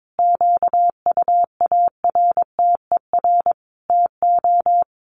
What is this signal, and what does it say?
Morse code, heard as a keyed tone, about 22 wpm, 700 Hz. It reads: QUARTER TO
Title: Sentences from Top 1300 Words 22wpm